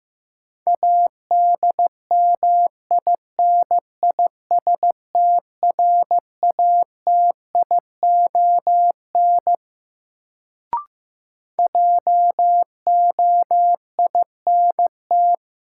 Morse code, heard as a keyed tone, about 15 wpm, 700 Hz. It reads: ADMINISTRATION JOINT
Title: Sentences from Top 1300 Words - New Word Drill Alt - 15wpm